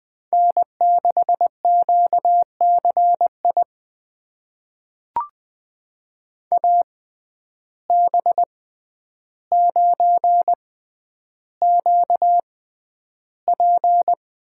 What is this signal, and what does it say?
Morse code, heard as a keyed tone, about 20 wpm, spaced as for 10 wpm, 700 Hz. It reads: N6QCI AB9QP
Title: USA Call Signs - 5 Characters - 6x Character Spacing - 20wpm